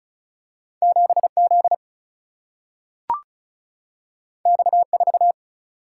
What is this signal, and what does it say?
Morse code, heard as a keyed tone, about 36 wpm, 700 Hz. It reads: 7 Z X 4
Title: Single-Letter Single-Number 35wpm